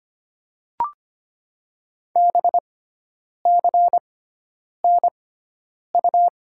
Morse code, heard as a keyed tone, about 25 wpm, spaced as for 9 wpm, 700 Hz. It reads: BCNU
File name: Top 90 QSO Elements - 6x Character Spacing - 25wpm